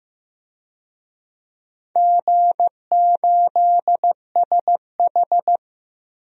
Morse code, heard as a keyed tone, about 15 wpm, 700 Hz. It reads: G8SH
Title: DXCC-100 Call Signs - 4 Characters - 15wpm